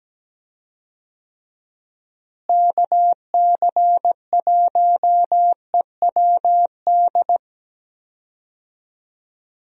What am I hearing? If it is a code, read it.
Morse code, heard as a keyed tone, about 17 wpm, 700 Hz. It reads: KC1EWD
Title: USA Call Signs - 6 Characters - Repeated 2x - 17wpm